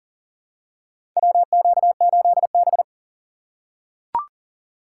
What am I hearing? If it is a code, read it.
Morse code, heard as a keyed tone, about 40 wpm, 700 Hz. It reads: WQ8B
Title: USA Call Signs - 4 Characters - 4x Character Spacing - 40wpm